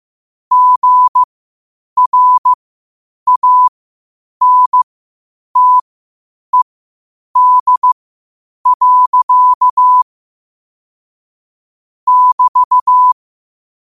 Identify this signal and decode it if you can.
Morse code, heard as a keyed tone, about 15 wpm, spaced as for 8 wpm, 1000 Hz. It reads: GRANTED. <BT>
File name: qotdfiles / 8wpm